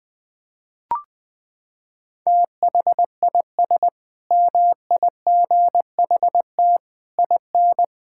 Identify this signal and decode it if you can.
Morse code, heard as a keyed tone, about 20 wpm, 700 Hz. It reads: THIS MIGHT IN
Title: Sentences from Top 1000 Words - Review Part 5 - 20wpm